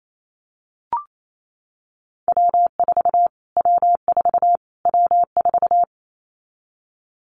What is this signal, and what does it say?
Morse code, heard as a keyed tone, about 28 wpm, 700 Hz. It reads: W4 W4 W4
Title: Callsign Prefixes for Top 25 DXCC - Repeated 3x - 28wpm